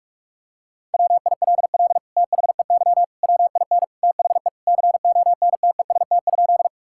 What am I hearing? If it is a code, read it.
Morse code, heard as a keyed tone, about 46 wpm, 700 Hz. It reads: WILL THEY WIN THE CONTEST?
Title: Sentences from Top 1600 Words 45wpm